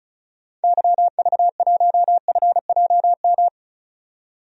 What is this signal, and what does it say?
Morse code, heard as a keyed tone, about 36 wpm, 700 Hz. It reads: YV1FJM
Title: DXCC-100 Call Signs - 6 Characters - 35wpm